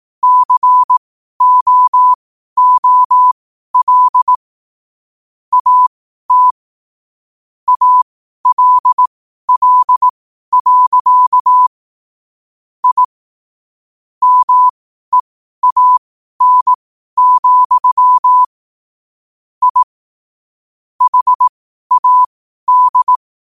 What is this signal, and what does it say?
Morse code, heard as a keyed tone, about 18 wpm, spaced as for 12 wpm, 1000 Hz. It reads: COOL AT ALL. I MEAN, I HAD
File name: qotdfiles / 12wpm